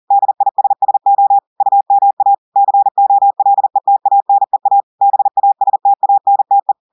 Morse code, heard as a keyed tone, about 40 wpm, 800 Hz. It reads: DISSO UMA COLETANEA BASTANTE
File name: audios / portugais / portugais texte en clair 1 vitesse 40 mots minutes